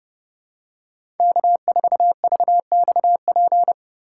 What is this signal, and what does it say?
Morse code, heard as a keyed tone, about 31 wpm, 700 Hz. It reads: K4VXP
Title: USA Call Signs - 5 Characters - 3x Character Spacing - 30wpm